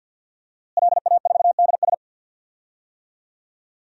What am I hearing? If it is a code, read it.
Morse code, heard as a keyed tone, about 53 wpm, 700 Hz. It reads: LA4DS
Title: DXCC-100 Call Signs - 5 Characters - Repeated 3x - 50wpm